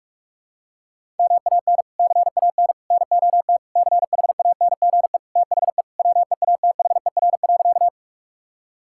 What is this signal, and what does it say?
Morse code, heard as a keyed tone, about 46 wpm, 700 Hz. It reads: MAN CAN NOT CHANGE THE WEATHER.
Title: Sentences from Top 1000 Words - Review Part 5 - 45wpm